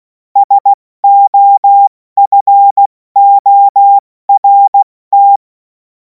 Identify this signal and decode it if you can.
Morse code, heard as a keyed tone, about 16 wpm, spaced as for 14 wpm, 800 Hz. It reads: SOFORT
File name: Morsekurs / Witze / Witz 07. Juli 2026 70 F80